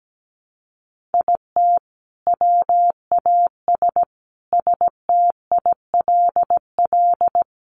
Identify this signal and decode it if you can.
Morse code, heard as a keyed tone, about 17 wpm, 700 Hz. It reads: IT WAS STILL
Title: Sentences from Top 1300 Words 17wpm